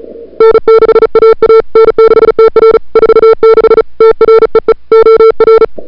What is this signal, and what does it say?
Morse code, heard as a keyed tone, about 36 wpm, 440 Hz. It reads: N6AA N6TR 46 TREE OR
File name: CWSprint / February 2007 CW Sprint / 01-2007 CW Sprint Hour1